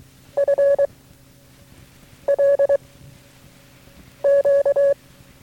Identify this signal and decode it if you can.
Morse code, heard as a keyed tone, about 23 wpm, 560 Hz. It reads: F L Q
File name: FreeFiles / CW-practice / KB6MT-tape3-side1